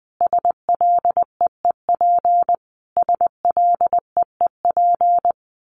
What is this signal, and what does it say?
Morse code, heard as a keyed tone, about 19 wpm, 700 Hz. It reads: SLEEP SLEEP
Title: Top 500 Words Repeated 20wpm